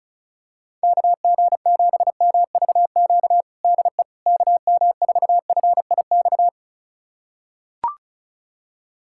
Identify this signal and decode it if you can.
Morse code, heard as a keyed tone, about 36 wpm, 700 Hz. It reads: KG7MVQ DE KM4FIX
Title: USA Callsign DE Callsign 35wpm